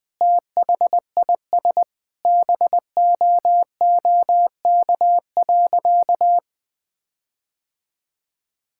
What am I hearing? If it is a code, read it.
Morse code, heard as a keyed tone, about 20 wpm, 700 Hz. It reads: THIS BOOK.
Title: Sentences with 7-Word Phrases - Spoken Beginning - 20wpm